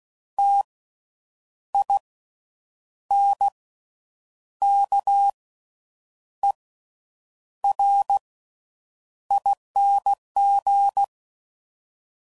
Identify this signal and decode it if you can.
Morse code, heard as a keyed tone, about 16 wpm, spaced as for 7 wpm, 800 Hz. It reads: TINKERING
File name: Slow With Faster Ings